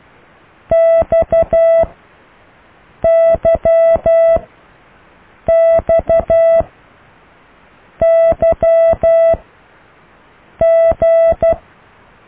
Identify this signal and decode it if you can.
Morse code, heard as a keyed tone, about 12 wpm, spaced as for 5 wpm, 660 Hz. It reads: XYXYG